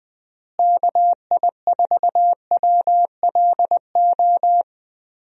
Morse code, heard as a keyed tone, about 20 wpm, 700 Hz. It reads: KI4WLO
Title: USA Call Signs - 6 Characters - Repeated 2x - 20wpm